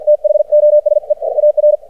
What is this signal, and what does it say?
Morse code, heard as a keyed tone, about 53 wpm, spaced as for 47 wpm, 600 Hz. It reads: THOSE AR
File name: archived / CRSnet